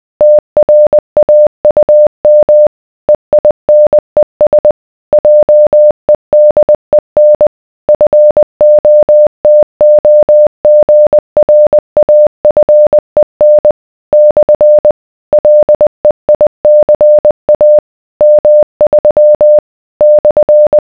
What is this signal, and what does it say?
Morse code, heard as a keyed tone, about 20 wpm, 600 Hz. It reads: TRAUM EINES JEDEN FOTOGRAFEN / LEICA M3 /